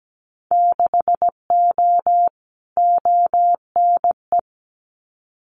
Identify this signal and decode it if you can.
Morse code, heard as a keyed tone, about 17 wpm, 700 Hz. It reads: 6O ONE
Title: Winter Field Day Exchanges Encore 17wpm